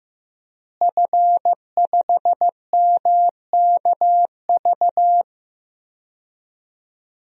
Morse code, heard as a keyed tone, about 16 wpm, 700 Hz. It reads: F5MKV
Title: DXCC-100 Call Signs - 5 Characters - Repeated 3x - 15wpm